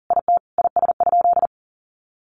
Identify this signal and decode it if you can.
Morse code, heard as a keyed tone, about 40 wpm, 700 Hz. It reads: IT IS?
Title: Sentences from Top 100 Words 40wpm